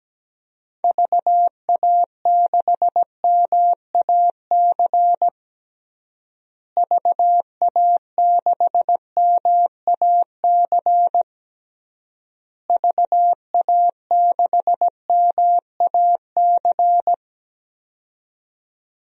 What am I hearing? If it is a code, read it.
Morse code, heard as a keyed tone, about 17 wpm, 700 Hz. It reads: VA6MAC VA6MAC VA6MAC
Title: DXCC-100 Call Signs - 6 Characters - Repeated 3x - 17wpm